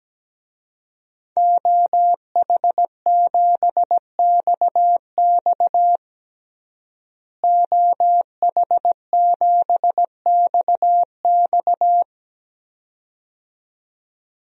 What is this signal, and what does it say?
Morse code, heard as a keyed tone, about 17 wpm, 700 Hz. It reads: OH7XX OH7XX
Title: DXCC-100 Call Signs - 5 Characters - Repeated 2x - 17wpm